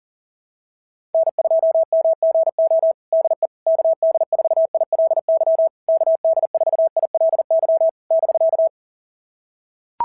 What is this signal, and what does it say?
Morse code, heard as a keyed tone, about 40 wpm, 650 Hz. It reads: N2MGO DE KD4ILY KD4ILY <BK>
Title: SOTA QSO Practice 40wpm